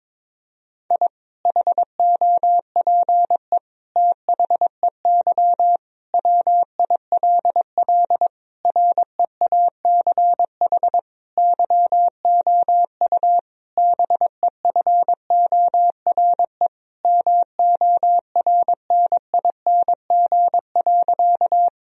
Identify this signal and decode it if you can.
Morse code, heard as a keyed tone, about 22 wpm, 700 Hz. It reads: I HOPE THEY WILL REACH YOU BEFORE MORNING.
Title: Sentences from Top 1000 Words - Review Part 2 - 22wpm